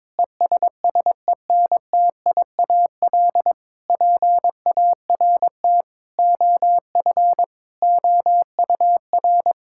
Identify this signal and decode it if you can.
Morse code, heard as a keyed tone, about 22 wpm, 700 Hz. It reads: ESSENTIAL PART OF OUR
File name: Difficult Sentences with Hints - 22wpm